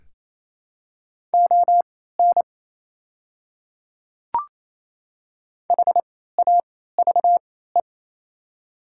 Morse code, heard as a keyed tone, about 28 wpm, spaced as for 16 wpm, 700 Hz. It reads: ON HAVE
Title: 3x Character Spacing - Top 300 Words - 28wpm